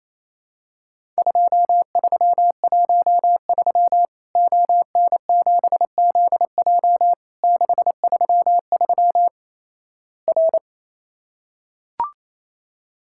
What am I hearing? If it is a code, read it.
Morse code, heard as a keyed tone, about 28 wpm, 700 Hz. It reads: 2313 ON7ZJ 633 R
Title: Worked All Europe DX Contest - QTC Exchange - 28wpm